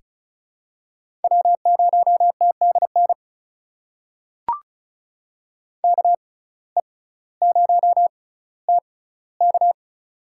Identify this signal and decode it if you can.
Morse code, heard as a keyed tone, about 36 wpm, spaced as for 18 wpm, 700 Hz. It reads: W0TDN KE0TK
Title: USA Call Signs - 5 Characters - 6x Character Spacing - 35wpm